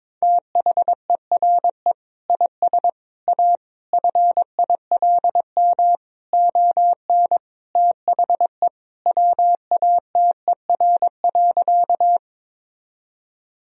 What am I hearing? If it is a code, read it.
Morse code, heard as a keyed tone, about 22 wpm, 700 Hz. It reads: THERE IS A FILM ON THE WATER.
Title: Sentences from Top 1000 Words - Review Part 3 - 22wpm